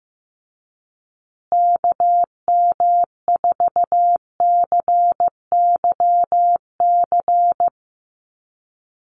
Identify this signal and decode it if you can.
Morse code, heard as a keyed tone, about 15 wpm, 700 Hz. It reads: KM4CYC
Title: USA Call Signs Repeated 3x 15wpm